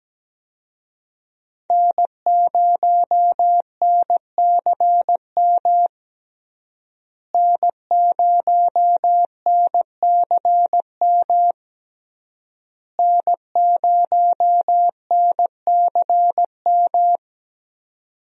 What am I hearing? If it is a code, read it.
Morse code, heard as a keyed tone, about 17 wpm, 700 Hz. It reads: N0NCM N0NCM N0NCM
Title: USA Call Signs - 5 Characters - Repeated 3x - 17wpm